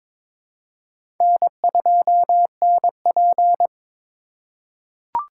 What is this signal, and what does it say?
Morse code, heard as a keyed tone, about 23 wpm, 700 Hz. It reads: N2NP
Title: USA Call Signs - 4 Characters - 2x Character Spacing - 22wpm